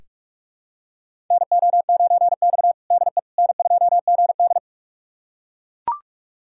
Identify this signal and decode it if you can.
Morse code, heard as a keyed tone, about 42 wpm, 700 Hz. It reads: NO9X DE N2GD
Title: USA Callsign DE Callsign - 4 Characters - 45wpm